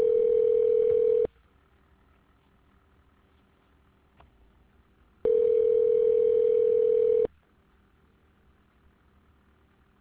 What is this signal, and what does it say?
Ringback tone (North America), 440 + 480 Hz.